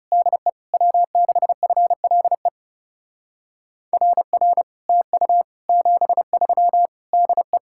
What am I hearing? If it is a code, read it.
Morse code, heard as a keyed tone, about 40 wpm, 700 Hz. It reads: DE W6FLE RR TU 73 DE